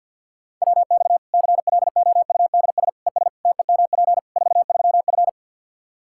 Morse code, heard as a keyed tone, about 49 wpm, 700 Hz. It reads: WX CLOUDS ES TEMP 43F
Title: Practice QSO - Exchange 2 - Info - 50wpm